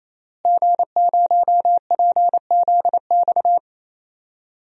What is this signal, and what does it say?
Morse code, heard as a keyed tone, about 28 wpm, 700 Hz. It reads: G0PZX
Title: DXCC-100 Call Signs - 5 Characters - Repeated 3x - 28wpm